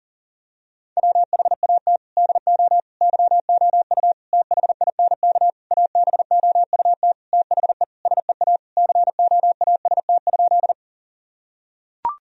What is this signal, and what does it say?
Morse code, heard as a keyed tone, about 37 wpm, 700 Hz. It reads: WHAT DO YOU THINK ABOUT THE SEA COAST?
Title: Sentences from Top 1000 Words - Review Part 4 - 40wpm